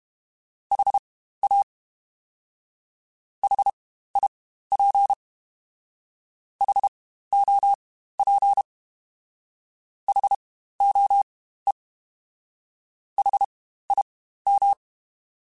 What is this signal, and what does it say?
Morse code, heard as a keyed tone, about 32 wpm, spaced as for 15 wpm, 800 Hz. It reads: HA HIP HOP HOE HIM
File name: Spaced Alphas / Spaced Alphas mp3s / Hs sp @30